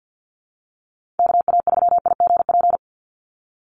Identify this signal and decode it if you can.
Morse code, heard as a keyed tone, about 50 wpm, 700 Hz. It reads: KA3IGP
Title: USA Call Signs 50wpm